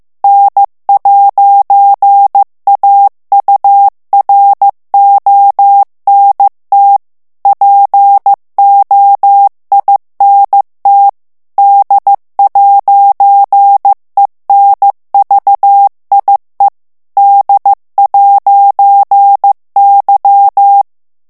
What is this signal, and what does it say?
Morse code, heard as a keyed tone, about 15 wpm, 800 Hz. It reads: N'AURONT POINT D'ENVIE D'Y
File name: FRANCAIS TEXTE CLAIR 2 VITESSE 14 MOTS MINUTES